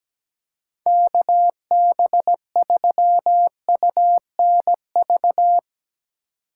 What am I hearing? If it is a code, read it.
Morse code, heard as a keyed tone, about 17 wpm, 700 Hz. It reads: KB3UNV
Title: USA Call Signs - 6 Characters - 4x Character Spacing - 17wpm